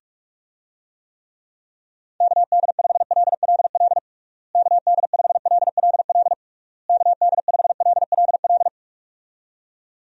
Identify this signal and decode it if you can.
Morse code, heard as a keyed tone, about 41 wpm, 700 Hz. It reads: KD5LLL KD5LLL KD5LLL
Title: USA Call Signs - 6 Characters - Repeated 3x - 45wpm